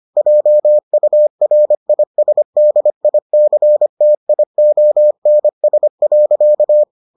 Morse code, heard as a keyed tone, about 25 wpm, 600 Hz. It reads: JURISDICTIONS.